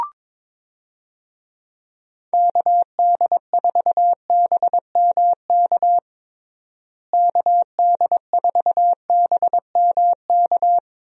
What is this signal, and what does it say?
Morse code, heard as a keyed tone, about 22 wpm, 700 Hz. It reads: KD4BMK KD4BMK
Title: USA Call Signs - 6 Characters - Repeated 3x - 22wpm